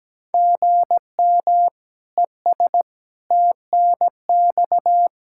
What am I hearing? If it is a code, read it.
Morse code, heard as a keyed tone, about 17 wpm, 700 Hz. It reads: GM ES TNX
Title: K1USN SST QSO Practice 17wpm